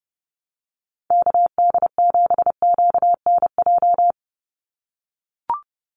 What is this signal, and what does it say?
Morse code, heard as a keyed tone, about 30 wpm, 700 Hz. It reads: KD7QNJ
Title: USA Call Signs 30wpm